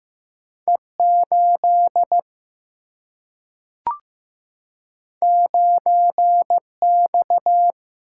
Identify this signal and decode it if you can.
Morse code, heard as a keyed tone, about 15 wpm, 700 Hz. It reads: E8 9X
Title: Single-Letter Single-Number 15wpm